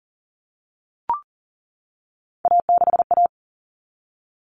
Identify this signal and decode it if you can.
Morse code, heard as a keyed tone, about 40 wpm, 700 Hz. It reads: A6A
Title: Common Cut Numbers - 3-Digit 40wpm